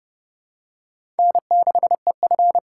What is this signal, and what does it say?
Morse code, heard as a keyed tone, about 30 wpm, 700 Hz. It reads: N6EF
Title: USA Call Signs - 4x Character Spacing - 30wpm